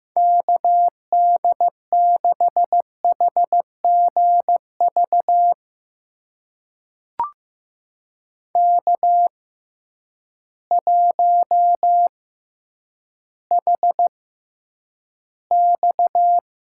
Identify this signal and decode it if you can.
Morse code, heard as a keyed tone, about 15 wpm, spaced as for 9 wpm, 700 Hz. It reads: KD6HGV K1HX
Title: USA Call Signs - 6x Character Spacing - 15wpm